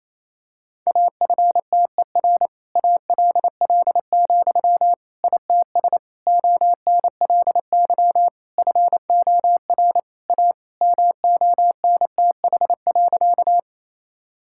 Morse code, heard as a keyed tone, about 27 wpm, 700 Hz. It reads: AFTER ALL, ITS ONLY FOR A MONTH.
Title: Sentences from Top 1000 Words - Review Part 5 - 28wpm